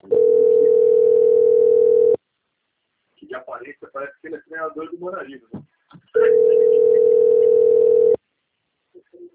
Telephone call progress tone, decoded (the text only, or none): ringback tone